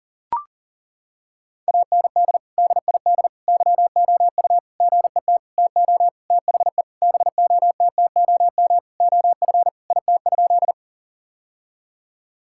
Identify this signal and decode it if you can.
Morse code, heard as a keyed tone, about 40 wpm, 700 Hz. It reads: AND DID YOU GET TO THE BOTTOM OF IT?
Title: Sentences from Top 1000 Words - Review Part 6 - 40wpm